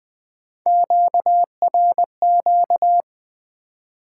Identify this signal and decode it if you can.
Morse code, heard as a keyed tone, about 20 wpm, 700 Hz. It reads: QRQ
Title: Top 90 QSO Elements 20wpm